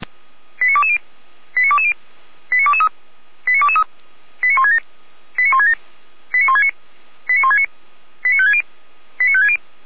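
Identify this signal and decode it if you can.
Five-tone selective call (ZVEI), heard as quick runs of five tones, tones of about 70 ms. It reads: 89220 89220 89223 89223 89167 89167 89169 89169 89560 89560
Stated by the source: Five-tone ZVEI selective call received on an SDR.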